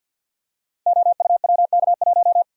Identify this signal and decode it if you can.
Morse code, heard as a keyed tone, about 52 wpm, 700 Hz. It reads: OUWK1
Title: Code Groups - 5 Letter:Numbers - 50wpm